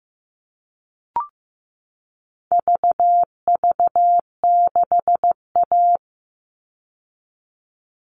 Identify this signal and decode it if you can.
Morse code, heard as a keyed tone, about 15 wpm, 700 Hz. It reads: VV6A
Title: All Cut Numbers - 4-Digit 15wpm